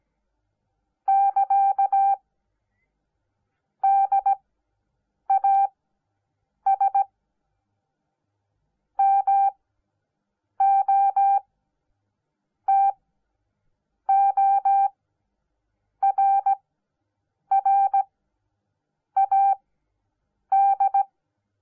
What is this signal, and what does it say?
Morse code, heard as a keyed tone, about 17 wpm, spaced as for 7 wpm, 790 Hz. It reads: <KA> DAS MOTORRAD